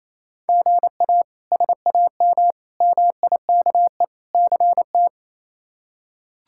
Morse code, heard as a keyed tone, about 28 wpm, 700 Hz. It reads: GA SAM MIKE CT
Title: K1USN SST QSO Practice - 28wpm